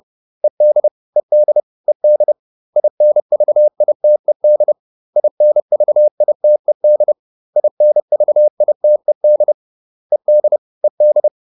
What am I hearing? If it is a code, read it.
Morse code, heard as a keyed tone, about 30 wpm, 600 Hz. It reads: ED ED ED INVITED INVITED INVITED ED ED